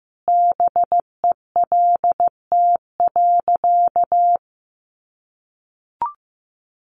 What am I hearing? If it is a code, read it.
Morse code, heard as a keyed tone, about 15 wpm, 700 Hz. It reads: BELT.
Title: Sentences from Top 1400 Words 15wpm